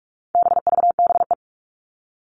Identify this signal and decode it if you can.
Morse code, heard as a keyed tone, about 44 wpm, 700 Hz. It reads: BVBE
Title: All Cut Numbers - 4-Digit 45wpm